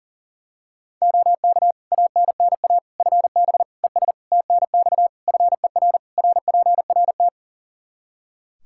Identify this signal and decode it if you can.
Morse code, heard as a keyed tone, about 37 wpm, 700 Hz. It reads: OK ANNA FB ES TNX FER RPRT
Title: Practice QSO - Exchange 2 - Info - 40wpm